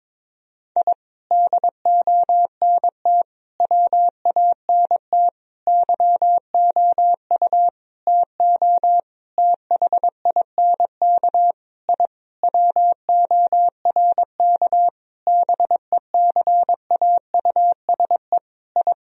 Morse code, heard as a keyed tone, about 22 wpm, 700 Hz. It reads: I DONT WANT YOU TO THINK I WORK BECAUSE I
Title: Sentences from Top 1000 Words - Review Part 3 - 22wpm